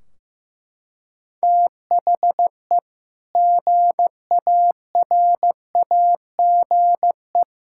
Morse code, heard as a keyed tone, about 15 wpm, 700 Hz. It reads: THE GARAGE
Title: Sentences from Top 1600 Words 15wpm